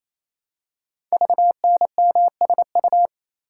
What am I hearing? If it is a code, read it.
Morse code, heard as a keyed tone, about 27 wpm, 700 Hz. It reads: VNMSU
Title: Code Groups - 5 Letters - 28wpm